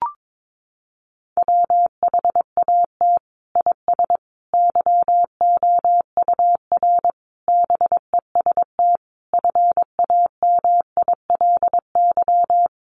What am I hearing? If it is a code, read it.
Morse code, heard as a keyed tone, about 22 wpm, 700 Hz. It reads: WHAT IS YOUR BEST FAMILY
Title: Sentences from Top 1200 Words 22wpm